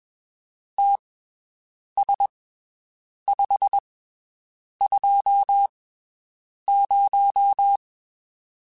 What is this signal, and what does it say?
Morse code, heard as a keyed tone, about 21 wpm, spaced as for 7 wpm, 800 Hz. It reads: TS520
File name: (spaced) / QSOs Spaced-mp3s / WD0FBT 1 @20 sp